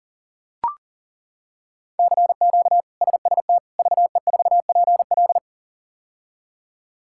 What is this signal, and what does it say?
Morse code, heard as a keyed tone, about 40 wpm, 700 Hz. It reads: CQ SST VE4PL
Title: K1USN SST QSO Practice - 40wpm